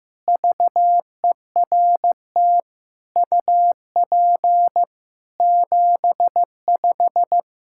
Morse code, heard as a keyed tone, about 16 wpm, 700 Hz. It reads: VERT UP 75
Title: K1USN SST QSO Practice 15wpm